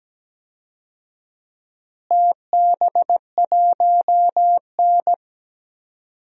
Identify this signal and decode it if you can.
Morse code, heard as a keyed tone, about 17 wpm, 700 Hz. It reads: TB1N
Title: DXCC-100 Call Signs - 4 Characters - Repeated 3x - 17wpm